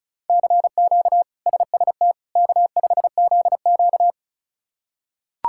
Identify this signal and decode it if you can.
Morse code, heard as a keyed tone, about 36 wpm, 700 Hz. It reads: CQ SST K5ZQ
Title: K1USN SST QSO Practice - 35wpm